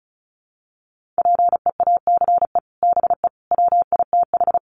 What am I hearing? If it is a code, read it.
Morse code, heard as a keyed tone, about 34 wpm, 700 Hz. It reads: PEACE BE WITH
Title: Sentences from Top 1500 Words 35wpm